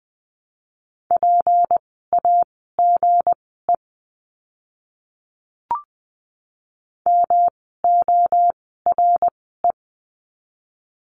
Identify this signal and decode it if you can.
Morse code, heard as a keyed tone, about 20 wpm, spaced as for 15 wpm, 700 Hz. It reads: PAGE MORE
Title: 160 Lesson 26f - Letter g - Words 20wpm